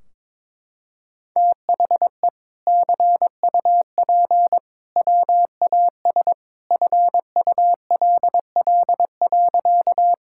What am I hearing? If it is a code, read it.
Morse code, heard as a keyed tone, about 22 wpm, 700 Hz. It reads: THE CUP WAS FULL.
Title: Sentences from Top 1600 Words 22wpm